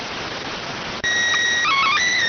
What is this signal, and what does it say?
SSTV (slow-scan television), PD 290.